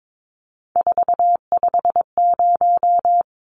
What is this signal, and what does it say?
Morse code, heard as a keyed tone, about 22 wpm, 700 Hz. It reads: 450
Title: Numbers - 3-Digit 22wpm